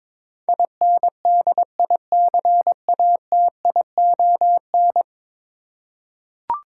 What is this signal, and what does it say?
Morse code, heard as a keyed tone, about 22 wpm, 700 Hz. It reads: INDICATION
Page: morse-code-ninja-practice-22wpm